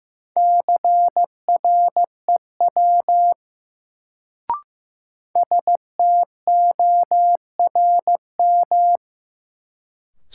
Morse code, heard as a keyed tone, about 15 wpm, 700 Hz. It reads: CREW STORM
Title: Sentences from Top 1500 Words - New Word Drill - 15wpm